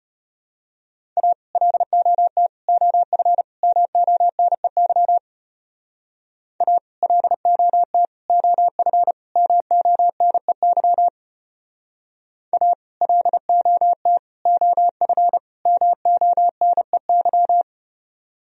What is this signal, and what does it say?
Morse code, heard as a keyed tone, about 33 wpm, 700 Hz. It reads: A LOT OF MONEY A LOT OF MONEY A LOT OF MONEY